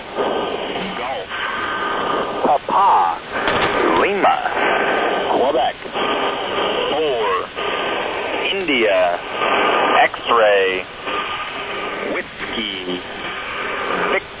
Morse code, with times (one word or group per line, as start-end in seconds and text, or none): none